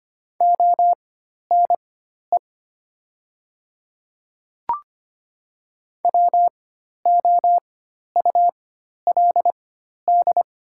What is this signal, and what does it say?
Morse code, heard as a keyed tone, about 25 wpm, spaced as for 12 wpm, 690 Hz. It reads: ONE WOULD
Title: 4x Character Spacing - Top 200 Words - 25wpm